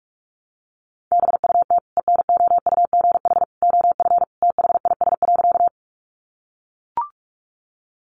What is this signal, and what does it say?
Morse code, heard as a keyed tone, about 44 wpm, 700 Hz. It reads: BUT ENOUGH OF THIS.
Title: Sentences from Top 400 Words 45wpm